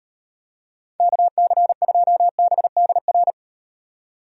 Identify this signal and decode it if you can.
Morse code, heard as a keyed tone, about 38 wpm, 700 Hz. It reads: KC2BDR